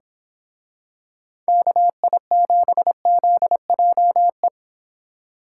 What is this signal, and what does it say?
Morse code, heard as a keyed tone, about 27 wpm, 700 Hz. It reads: KI7ZJE
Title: Speed Racing - USA 2x3 Call Signs 22wpm